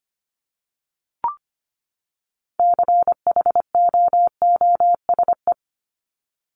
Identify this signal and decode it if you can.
Morse code, heard as a keyed tone, about 25 wpm, 700 Hz. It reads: CHOOSE
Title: Top 501-750 Words 25wpm